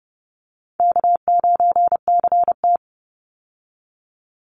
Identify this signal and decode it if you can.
Morse code, heard as a keyed tone, about 30 wpm, 700 Hz. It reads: K9CT
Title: Call Signs Repeated 3x - 30wpm